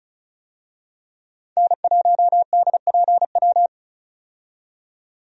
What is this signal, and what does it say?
Morse code, heard as a keyed tone, about 36 wpm, 700 Hz. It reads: N1DPW
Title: USA Call Signs - 5 Characters - 35wpm